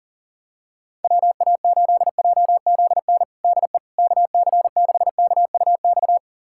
Morse code, heard as a keyed tone, about 40 wpm, 700 Hz. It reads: WA8JZN DE KC6KUX
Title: USA Callsign DE Callsign - 6 Characters - 40wpm